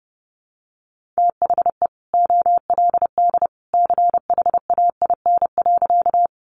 Morse code, heard as a keyed tone, about 30 wpm, 700 Hz. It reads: THE OLD CHAIN.
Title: Sentences from Top 1500 Words 30wpm